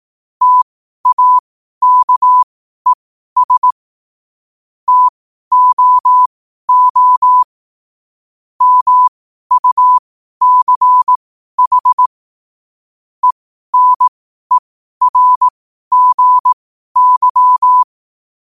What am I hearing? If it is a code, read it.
Morse code, heard as a keyed tone, about 18 wpm, spaced as for 12 wpm, 1000 Hz. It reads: TAKES TOO MUCH ENERGY
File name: qotdfiles / 12wpm